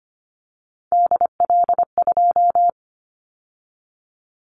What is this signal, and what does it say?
Morse code, heard as a keyed tone, about 25 wpm, 700 Hz. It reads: DL2
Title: Callsign Prefixes for Top 25 DXCC - 25wpm